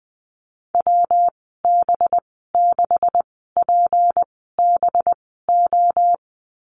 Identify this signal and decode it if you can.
Morse code, heard as a keyed tone, about 20 wpm, spaced as for 14 wpm, 700 Hz. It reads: WB6PBO
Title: 245 Lesson 39g - Letter z - Call Signs 20wpm